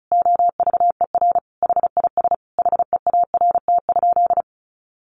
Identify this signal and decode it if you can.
Morse code, heard as a keyed tone, about 34 wpm, 700 Hz. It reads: OVER HIS HEART?
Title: Sentences from Top 1200 Words 35wpm